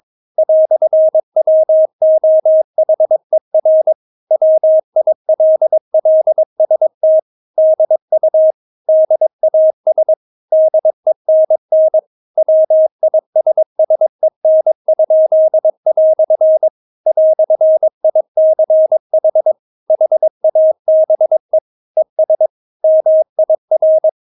Morse code, heard as a keyed tone, about 22 wpm, 630 Hz. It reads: "WOHER WILLST DU DAS DENN WISSEN?" "ICH HABE ES MIR
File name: Witz3 22wpm